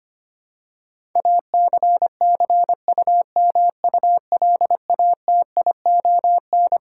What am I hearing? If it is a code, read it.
Morse code, heard as a keyed tone, about 25 wpm, 700 Hz. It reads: ACCUMULATION
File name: Twelve Letter Words - 25wpm